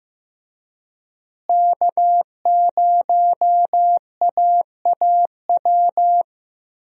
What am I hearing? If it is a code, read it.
Morse code, heard as a keyed tone, about 15 wpm, 700 Hz. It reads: K0AAW
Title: USA Call Signs - 5 Characters - Repeated 2x - 15wpm